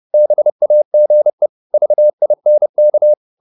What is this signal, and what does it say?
Morse code, heard as a keyed tone, about 30 wpm, 600 Hz. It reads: BAGE VINK
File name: Korte ord DK 30wpm